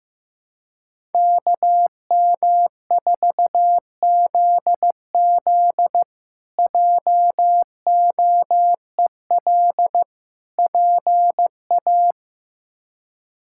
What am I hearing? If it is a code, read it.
Morse code, heard as a keyed tone, about 15 wpm, 700 Hz. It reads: KM4ZZ JOEL PA
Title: K1USN SST QSO Practice - 15wpm